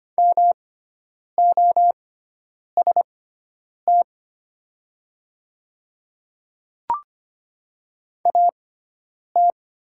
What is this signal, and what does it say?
Morse code, heard as a keyed tone, about 25 wpm, spaced as for 9 wpm, 700 Hz. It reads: MOST AT
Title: 6x Character Spacing - Top 100 Words - 25wpm